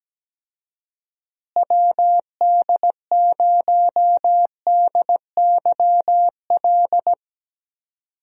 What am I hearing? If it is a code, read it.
Morse code, heard as a keyed tone, about 17 wpm, 700 Hz. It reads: WD0DYL
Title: USA Call Signs - 6 Characters - Repeated 2x - 17wpm